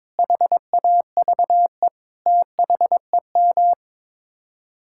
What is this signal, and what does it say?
Morse code, heard as a keyed tone, about 22 wpm, 700 Hz. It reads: HAVE THEM
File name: practice / 22wpm / Sets of 8 Words - 22wpm